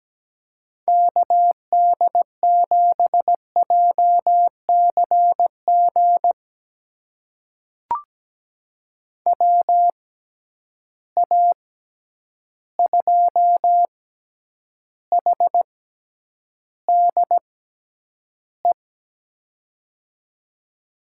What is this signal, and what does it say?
Morse code, heard as a keyed tone, about 17 wpm, spaced as for 9 wpm, 700 Hz. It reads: KD7JCG WA2HDE
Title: USA Call Signs - 6 Characters - 6x Character Spacing - 17wpm